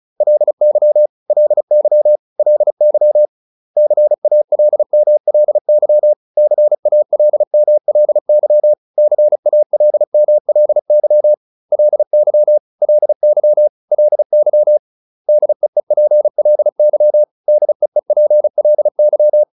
Morse code, heard as a keyed tone, about 34 wpm, 600 Hz. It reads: LY LY LY CALMLY CALMLY CALMLY LY LY LY DEEPLY DEEPLY